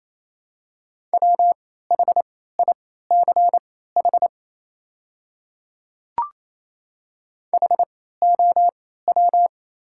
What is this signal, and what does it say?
Morse code, heard as a keyed tone, about 28 wpm, spaced as for 16 wpm, 700 Hz. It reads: WHICH HOW
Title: 3x Character Spacing - Top 200 Words - 28wpm